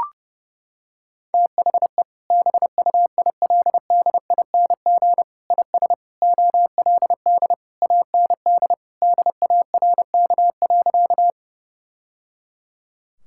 Morse code, heard as a keyed tone, about 30 wpm, 700 Hz. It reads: THE BUILDING IS OLD AND DARK.
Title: Sentences from Top 1000 Words - Review Part 5 - 30wpm